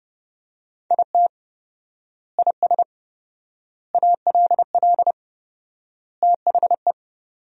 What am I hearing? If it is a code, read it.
Morse code, heard as a keyed tone, about 30 wpm, 700 Hz. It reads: IT IS ALL THE
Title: 4x Word-Spacing - Sentences from Top 200 Words - 30wpm